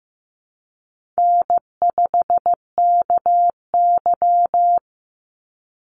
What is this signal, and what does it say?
Morse code, heard as a keyed tone, about 15 wpm, 700 Hz. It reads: N5KY
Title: Easy Call Signs - 15wpm